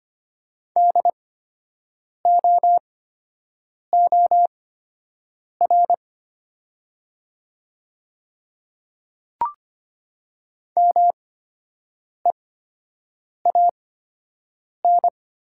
Morse code, heard as a keyed tone, about 25 wpm, spaced as for 7 wpm, 700 Hz. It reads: DOOR MEAN
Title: 8x Character Spacing - Top 300 Words - 25wpm